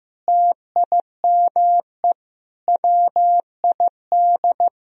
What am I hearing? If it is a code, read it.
Morse code, heard as a keyed tone, about 15 wpm, 700 Hz. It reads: TIME WID
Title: Sentences from Top 300 Words and Abbreviations 15wpm